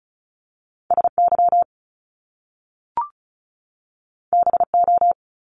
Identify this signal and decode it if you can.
Morse code, heard as a keyed tone, about 36 wpm, 700 Hz. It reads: S Y B O
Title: Top Two Letter Combinations in Words 35wpm